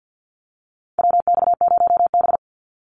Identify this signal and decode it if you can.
Morse code, heard as a keyed tone, about 50 wpm, 700 Hz. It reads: WX0B
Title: Call Signs - 50wpm